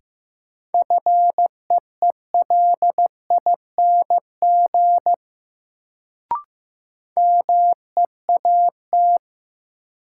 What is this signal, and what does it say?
Morse code, heard as a keyed tone, about 15 wpm, 700 Hz. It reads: FEELING MEAT
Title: Sentences from Top 1000 Words - Review Part 3 - New Word Drill - 15wpm